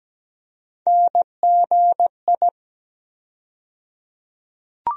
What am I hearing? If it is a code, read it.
Morse code, heard as a keyed tone, about 17 wpm, 700 Hz. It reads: NGI
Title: USA Call Sign Postfixes - 3 Letters 17wpm